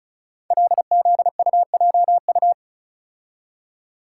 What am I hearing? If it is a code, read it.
Morse code, heard as a keyed tone, about 36 wpm, 700 Hz. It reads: LZUJU
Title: Code Groups - 5 Letters - 35wpm